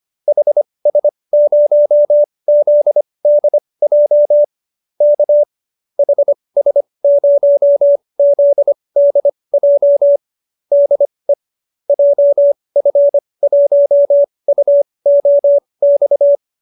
Morse code, heard as a keyed tone, about 25 wpm, spaced as for 20 wpm, 590 Hz. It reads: HS0ZDJ K HS0ZDJ DE JF1UOX
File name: qso216 20